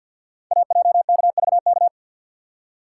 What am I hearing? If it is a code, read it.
Morse code, heard as a keyed tone, about 46 wpm, 700 Hz. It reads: AJKVK
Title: Code Groups - 5 Letters - 50wpm